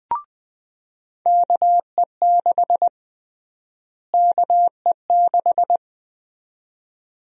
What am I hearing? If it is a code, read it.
Morse code, heard as a keyed tone, about 20 wpm, 700 Hz. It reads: KE6 KE6
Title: USA Call Sign Prefixes - Repeated 2x - 20wpm